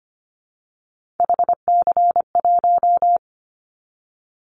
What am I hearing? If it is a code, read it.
Morse code, heard as a keyed tone, about 25 wpm, 700 Hz. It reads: HC1
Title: Callsign Prefixes for Top 100 DXCC - 25wpm